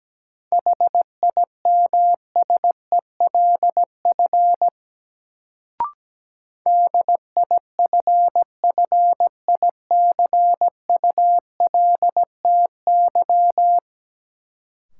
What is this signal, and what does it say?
Morse code, heard as a keyed tone, about 17 wpm, 700 Hz. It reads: HIMSELF DIFFICULTY
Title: Sentences from Top 1500 Words - New Word Drill Alt - 17wpm